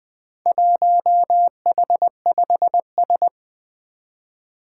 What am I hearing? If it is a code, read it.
Morse code, heard as a keyed tone, about 20 wpm, 700 Hz. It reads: 1H5S
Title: Four Challenging Letter-Number Combinations 20wpm